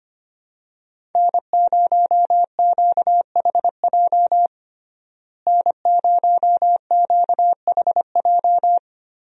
Morse code, heard as a keyed tone, about 25 wpm, 700 Hz. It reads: N0QHJ N0QHJ
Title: USA Call Signs - 5 Characters - Repeated 2x - 25wpm